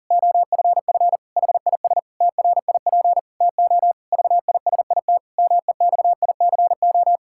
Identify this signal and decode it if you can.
Morse code, heard as a keyed tone, about 40 wpm, 700 Hz. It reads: OFF HIS TRIP TO VISIT MEXICO
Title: Pangram Challenge v2 - 40wpm